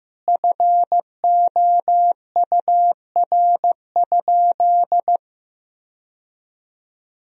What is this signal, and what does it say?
Morse code, heard as a keyed tone, about 15 wpm, 700 Hz. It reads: FOUR?
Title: Sentences from Top 1000 Words - Review Part 2 - 15wpm